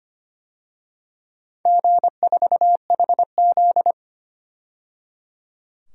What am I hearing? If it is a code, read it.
Morse code, heard as a keyed tone, about 25 wpm, 700 Hz. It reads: G4HZ
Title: DXCC-100 Call Signs - 4 Characters - 25wpm